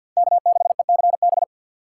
Morse code, heard as a keyed tone, about 48 wpm, 700 Hz. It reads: K6ECB
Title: Speed Racing - USA 1x3 Call Signs 50wpm